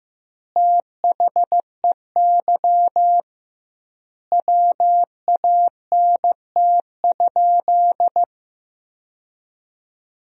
Morse code, heard as a keyed tone, about 15 wpm, 700 Hz. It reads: THEY WANT?
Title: 2x Word-Spacing - Sentences from Top 200 Words - 15wpm